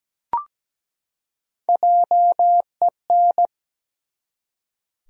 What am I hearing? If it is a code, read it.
Morse code, heard as a keyed tone, about 17 wpm, 700 Hz. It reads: JEN
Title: USA Common Names - 17wpm